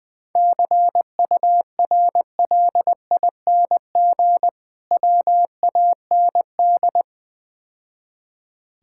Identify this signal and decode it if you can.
Morse code, heard as a keyed tone, about 20 wpm, 700 Hz. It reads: CURLING WAND
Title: Difficult Words - Speed-Racing with Hints - 20wpm